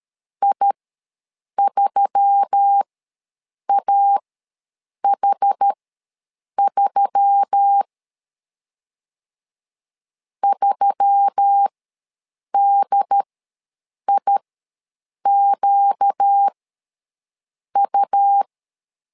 Morse code, heard as a keyed tone, about 13 wpm, spaced as for 7 wpm, 800 Hz. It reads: I3AH3 3DIQU
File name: LEZIONI / 4 Q1S3 ws 12wpm 2038